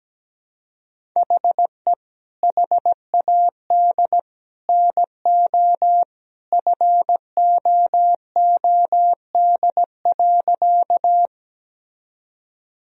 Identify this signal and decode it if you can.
Morse code, heard as a keyed tone, about 17 wpm, 700 Hz. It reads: HE HAD NO FOOD.
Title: Sentences from Top 1000 Words - Review Part 1 - 17wpm